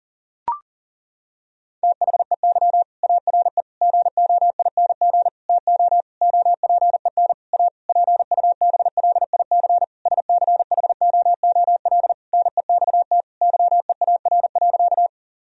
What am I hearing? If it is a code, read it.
Morse code, heard as a keyed tone, about 40 wpm, 700 Hz. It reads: THEY ARE GOING TO OPEN A PUBLIC SCHOOL NEXT YEAR.
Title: Sentences from Top 1000 Words - Review Part 3 - 40wpm